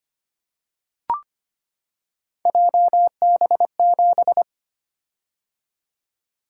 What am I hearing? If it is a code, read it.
Morse code, heard as a keyed tone, about 25 wpm, 700 Hz. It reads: JB7
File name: Three Letter-Number Combinations - 25wpm